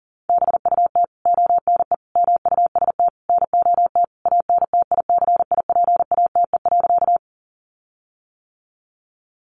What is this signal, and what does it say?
Morse code, heard as a keyed tone, about 40 wpm, 700 Hz. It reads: BUT ONE MUST NOT ANTICIPATE.
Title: Sentences from Top 1500 Words 40wpm